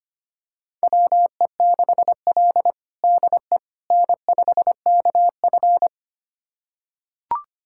Morse code, heard as a keyed tone, about 25 wpm, 700 Hz. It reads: WE6L DE N5KF
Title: USA Callsign DE Callsign - 4 Characters - 25wpm